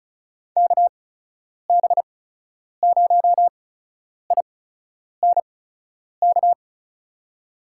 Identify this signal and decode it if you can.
Morse code, heard as a keyed tone, about 36 wpm, spaced as for 10 wpm, 700 Hz. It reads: KB0INK
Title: USA Call Signs - 6 Characters - 8x Character Spacing - 35wpm